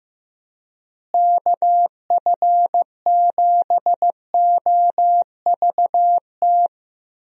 Morse code, heard as a keyed tone, about 15 wpm, 700 Hz. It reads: KF7OVT
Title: USA Call Signs - 6 Characters - Repeated 2x - 15wpm